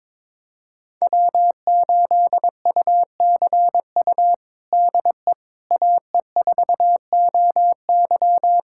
Morse code, heard as a keyed tone, about 22 wpm, 700 Hz. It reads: W8UCU DE AE4OY
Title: USA Callsign DE Callsign 22wpm